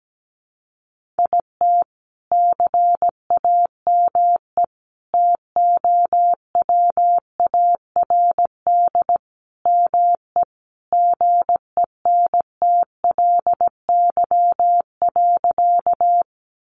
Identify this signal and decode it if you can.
Morse code, heard as a keyed tone, about 17 wpm, 700 Hz. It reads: IT CAME TOWARD ME GENTLY.
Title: Sentences from Top 1500 Words 17wpm